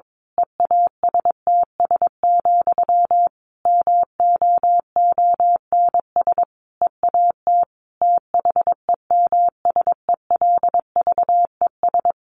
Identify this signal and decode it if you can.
Morse code, heard as a keyed tone, about 22 wpm, 700 Hz. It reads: EASTS, MOONS EAT THEMSELVES
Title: A Sound Of Thunder 22wpm